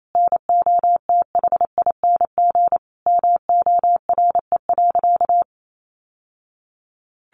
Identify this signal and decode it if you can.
Morse code, heard as a keyed tone, about 28 wpm, 700 Hz. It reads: NOTHING MORE.
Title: Sentences from Top 800 Words 28wpm